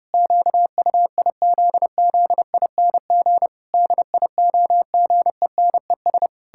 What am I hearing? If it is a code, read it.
Morse code, heard as a keyed tone, about 30 wpm, 700 Hz. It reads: QUIZZING DIOGENES
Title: Pangram Challenge v2 - 30wpm